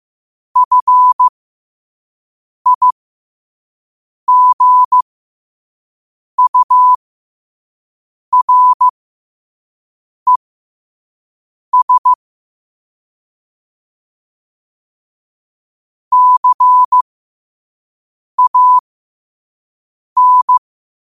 Morse code, heard as a keyed tone, about 15 wpm, spaced as for 5 wpm, 1000 Hz. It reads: FIGURES CAN
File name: qotdfiles / 5wpm